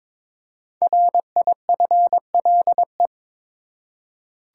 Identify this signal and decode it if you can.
Morse code, heard as a keyed tone, about 22 wpm, 700 Hz. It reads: RIFLE
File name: 5 Letter Words - First 3 Letters Given - 22wpm